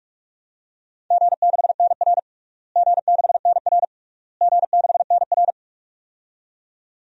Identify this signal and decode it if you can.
Morse code, heard as a keyed tone, about 46 wpm, 700 Hz. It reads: G6NR G6NR G6NR
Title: DXCC-100 Call Signs - 4 Characters - Repeated 3x - 45wpm